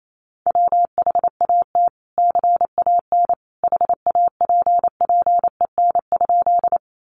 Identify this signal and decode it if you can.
Morse code, heard as a keyed tone, about 28 wpm, 700 Hz. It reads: WHAT CAN HAPPEN?
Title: Sentences from Top 400 Words 28wpm